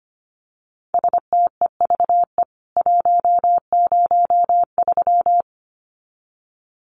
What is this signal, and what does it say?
Morse code, heard as a keyed tone, about 25 wpm, 700 Hz. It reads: STEVE 103
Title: CWT Exchange 25wpm